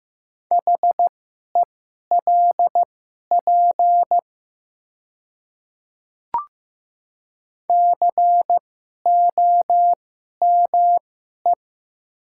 Morse code, heard as a keyed tone, about 15 wpm, spaced as for 11 wpm, 690 Hz. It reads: HELP COME
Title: 2x Character Spacing - Top 200 Words - 15wpm